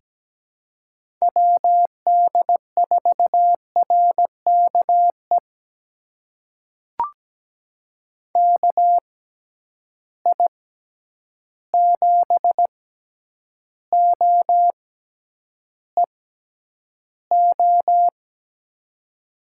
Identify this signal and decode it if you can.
Morse code, heard as a keyed tone, about 17 wpm, spaced as for 9 wpm, 700 Hz. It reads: WD4RKE KI7OEO
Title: USA Call Signs - 6 Characters - 6x Character Spacing - 17wpm